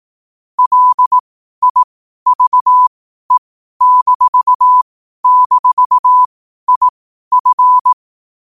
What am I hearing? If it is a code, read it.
Morse code, heard as a keyed tone, about 18 wpm, spaced as for 12 wpm, 1000 Hz. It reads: LIVE--IF
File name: qotdfiles / 12wpm